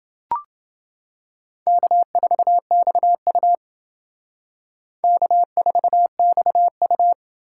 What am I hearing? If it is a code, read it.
Morse code, heard as a keyed tone, about 28 wpm, 700 Hz. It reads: K4XU K4XU